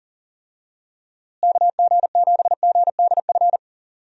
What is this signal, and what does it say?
Morse code, heard as a keyed tone, about 40 wpm, 700 Hz. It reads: KG7GDF
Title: USA Call Signs - 6 Characters - Repeated 3x - 40wpm